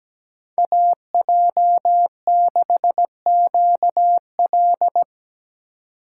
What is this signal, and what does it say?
Morse code, heard as a keyed tone, about 17 wpm, 700 Hz. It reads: AJ6QL
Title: K1USN SST QSO Practice - 17wpm